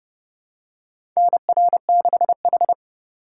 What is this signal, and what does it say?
Morse code, heard as a keyed tone, about 30 wpm, 700 Hz. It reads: NR6H
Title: USA Call Signs - 2x Character Spacing - 30wpm